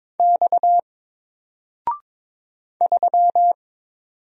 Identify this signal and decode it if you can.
Morse code, heard as a keyed tone, about 22 wpm, 700 Hz. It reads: X 3
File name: Single-Letter-Number-22wpm